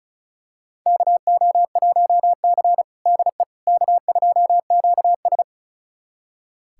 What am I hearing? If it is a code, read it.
Morse code, heard as a keyed tone, about 34 wpm, 700 Hz. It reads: KO1C DE K2QS
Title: USA Callsign DE Callsign - 4 Characters - 35wpm